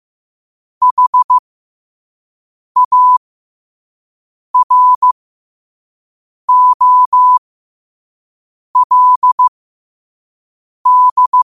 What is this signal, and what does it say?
Morse code, heard as a keyed tone, about 15 wpm, spaced as for 5 wpm, 1000 Hz. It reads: HAROLD
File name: qotdfiles / 5wpm